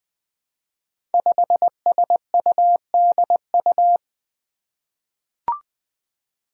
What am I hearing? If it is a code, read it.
Morse code, heard as a keyed tone, about 20 wpm, 700 Hz. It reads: 5SUDU
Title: Five Challenging Letter-Number Combinations 20wpm